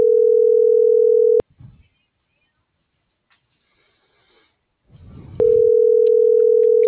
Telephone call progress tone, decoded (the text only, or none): ringback tone